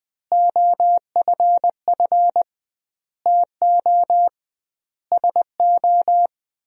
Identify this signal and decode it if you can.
Morse code, heard as a keyed tone, about 20 wpm, 700 Hz. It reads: OFF TO SO
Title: 2x Word-Spacing - Sentences from Top 400 Words - 20wpm